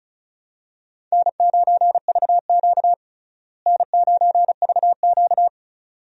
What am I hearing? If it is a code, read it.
Morse code, heard as a keyed tone, about 34 wpm, 700 Hz. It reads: N9VQ N9VQ
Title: USA Call Signs - 4 Characters - Repeated 2x - 35wpm